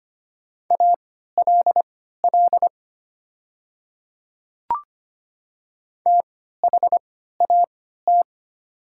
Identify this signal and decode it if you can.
Morse code, heard as a keyed tone, about 25 wpm, spaced as for 14 wpm, 700 Hz. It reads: ALL THAT
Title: 3x Character Spacing - Top 100 Words - 25wpm